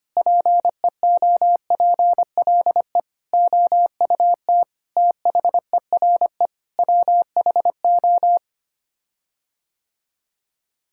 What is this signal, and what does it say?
Morse code, heard as a keyed tone, about 25 wpm, 700 Hz. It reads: PEOPLE OUT THERE WHO
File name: Sets of 9 Words - 25wpm